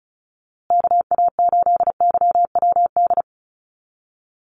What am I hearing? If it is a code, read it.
Morse code, heard as a keyed tone, about 34 wpm, 700 Hz. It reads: KA8YWD
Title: USA Call Signs Repeated 2x 35wpm